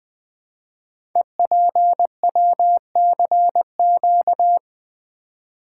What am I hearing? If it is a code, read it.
Morse code, heard as a keyed tone, about 20 wpm, 700 Hz. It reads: EPWCQ
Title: Code Groups - 5 Letter:Numbers - 20wpm